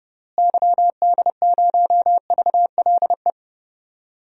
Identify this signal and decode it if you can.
Morse code, heard as a keyed tone, about 30 wpm, 700 Hz. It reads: YD0VLE
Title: DXCC-100 Call Signs - 6 Characters - Repeated 3x - 30wpm